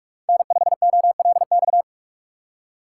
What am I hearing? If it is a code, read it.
Morse code, heard as a keyed tone, about 46 wpm, 700 Hz. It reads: N5OLX